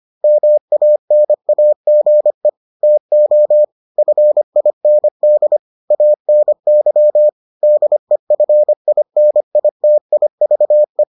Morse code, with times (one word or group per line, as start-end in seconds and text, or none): 0.2-2.5 MANAGE
2.8-3.6 TO
4.0-5.6 FIND
5.9-7.3 ANY
7.6-11.0 DEFINITIVE